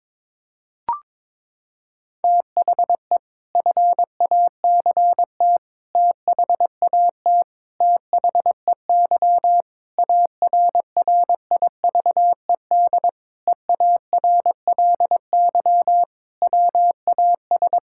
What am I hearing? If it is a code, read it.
Morse code, heard as a keyed tone, about 22 wpm, 700 Hz. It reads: THE FACT THAT THEY ARRIVED EARLY WAS
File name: practice / 22wpm / Sentences with 4-Word Phrases - 22wpm